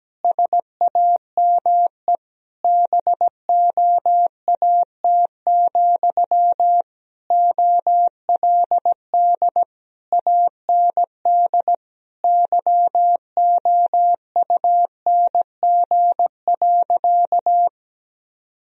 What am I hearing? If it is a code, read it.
Morse code, heard as a keyed tone, about 17 wpm, 700 Hz. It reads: SAME BOAT, OLD AND YOUNG.
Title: Sentences from Top 1000 Words - Review Part 3 - 17wpm